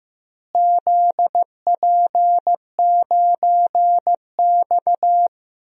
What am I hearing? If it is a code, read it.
Morse code, heard as a keyed tone, about 15 wpm, 700 Hz. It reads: ZP9X
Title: DXCC-100 Call Signs - 4 Characters - Repeated 2x - 15wpm